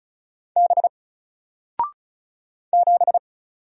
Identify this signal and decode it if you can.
Morse code, heard as a keyed tone, about 34 wpm, 700 Hz. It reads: B 7
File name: Single-Letter-Number-35wpm